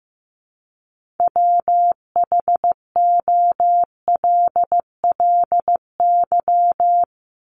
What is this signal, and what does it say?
Morse code, heard as a keyed tone, about 15 wpm, 700 Hz. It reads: WHOLLY
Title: LY Suffix - Top 100 Words with Base Words 15wpm